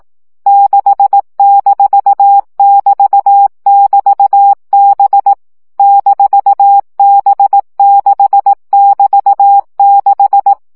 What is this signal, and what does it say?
Morse code, heard as a keyed tone, about 18 wpm, 800 Hz. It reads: 6-<BT><BT>B -B6<BT>6
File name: CW gyakorlas b 6 egyenlo minusz 15 baud